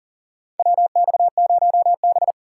Morse code, heard as a keyed tone, about 40 wpm, 700 Hz. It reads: WX0B
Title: USA Call Signs - 4 Characters - 6x Character Spacing - 40wpm